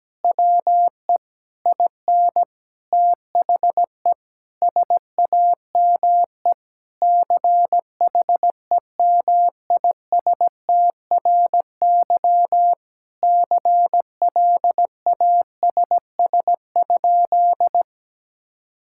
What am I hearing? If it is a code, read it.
Morse code, heard as a keyed tone, about 17 wpm, 700 Hz. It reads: WE IN THE SAME CHEMISTRY CLASS?
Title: Sentences from Top 1000 Words - Review Part 6 - 17wpm